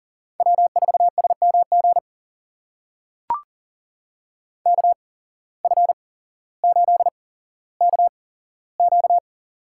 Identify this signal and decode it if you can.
Morse code, heard as a keyed tone, about 40 wpm, spaced as for 17 wpm, 700 Hz. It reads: W4SMG KF8KQ
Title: USA Call Signs - 5 Characters - 8x Character Spacing - 40wpm